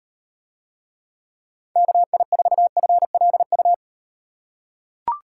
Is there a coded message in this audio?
Yes: Morse code from 1.8 to 3.7 s, heard as a keyed tone, about 38 wpm, 700 Hz. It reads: KI4FLU